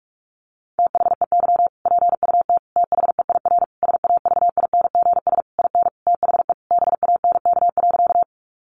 Morse code, heard as a keyed tone, about 44 wpm, 700 Hz. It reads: THEY PUT THEIR SAVINGS IN THE BANK.
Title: Sentences from Top 1200 Words 45wpm